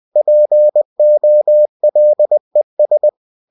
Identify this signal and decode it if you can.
Morse code, heard as a keyed tone, about 20 wpm, 600 Hz. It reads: POLES
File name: cw-books / winnie-the-pooh / winnie-20wpm-page-0093